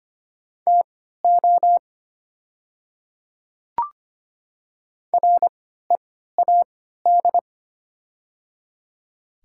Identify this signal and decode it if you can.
Morse code, heard as a keyed tone, about 25 wpm, spaced as for 14 wpm, 700 Hz. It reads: TO READ
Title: 3x Character Spacing - Top 300 Words - 25wpm